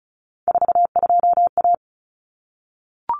Morse code, heard as a keyed tone, about 34 wpm, 700 Hz. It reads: 42A
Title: Common Cut Numbers - 3-Digit 35wpm